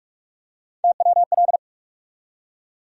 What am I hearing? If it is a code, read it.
Morse code, heard as a keyed tone, about 42 wpm, 700 Hz. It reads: TWL
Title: USA Call Sign Postfixes - 3 Letters 45wpm